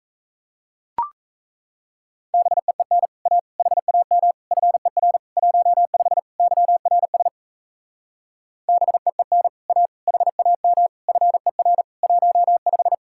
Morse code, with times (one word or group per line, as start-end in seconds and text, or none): 2.3-3.1 BEEN
3.3-3.4 A
3.6-4.3 HAM
4.5-5.2 FER
5.4-6.2 15
6.4-7.3 YRS
8.7-9.5 BEEN
9.7-9.9 A
10.1-10.9 HAM
11.1-11.8 FER
12.0-12.9 15